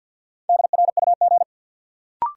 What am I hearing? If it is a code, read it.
Morse code, heard as a keyed tone, about 52 wpm, 700 Hz. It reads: DRUG
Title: Sets of 5 Words - New Word Drill - 50wpm